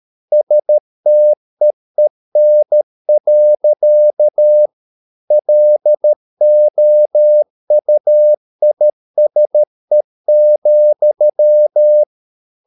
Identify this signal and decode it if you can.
Morse code, heard as a keyed tone, about 13 wpm, 600 Hz. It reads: STEEN. LOUISE,
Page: Ferie 13 wpm